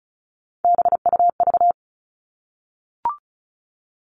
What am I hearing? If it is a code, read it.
Morse code, heard as a keyed tone, about 34 wpm, 700 Hz. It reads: BUV
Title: All Cut Numbers - 3-Digit 35wpm